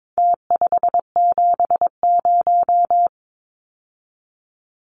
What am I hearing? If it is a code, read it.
Morse code, heard as a keyed tone, about 22 wpm, 700 Hz. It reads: T570
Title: Numbers - Contest Style - 4-Digit 22wpm